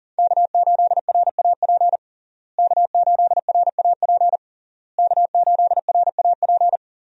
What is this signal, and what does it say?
Morse code, heard as a keyed tone, about 40 wpm, 700 Hz. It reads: K8RAP K8RAP K8RAP
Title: USA Call Signs - 5 Characters - Repeated 3x - 40wpm